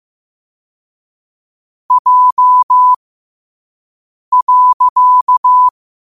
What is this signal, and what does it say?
Morse code, heard as a keyed tone, about 15 wpm, spaced as for 5 wpm, 1000 Hz. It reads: J.
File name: qotdfiles / 5wpm